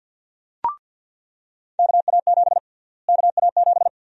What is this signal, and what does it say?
Morse code, heard as a keyed tone, about 47 wpm, 700 Hz. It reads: KA7 KA7
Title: USA Call Sign Prefixes - Repeated 2x - 50wpm